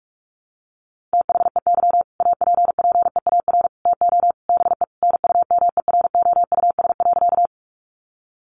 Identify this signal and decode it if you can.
Morse code, heard as a keyed tone, about 44 wpm, 700 Hz. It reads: THEY APPEAR TO BE NUMEROUS.
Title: Sentences from Top 1400 Words 45wpm